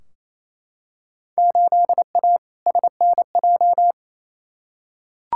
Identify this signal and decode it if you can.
Morse code, heard as a keyed tone, about 28 wpm, 700 Hz. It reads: 8A SNJ
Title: Field Day Exchanges Encore - 28wpm